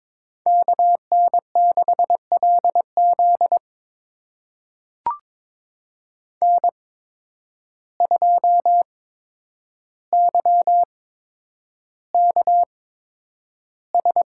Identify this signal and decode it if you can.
Morse code, heard as a keyed tone, about 22 wpm, spaced as for 9 wpm, 700 Hz. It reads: KN6LZ N2YKS
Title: USA Call Signs - 5 Characters - 8x Character Spacing - 22wpm